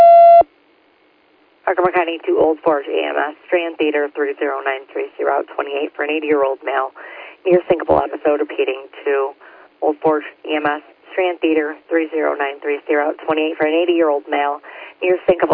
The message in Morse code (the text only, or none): none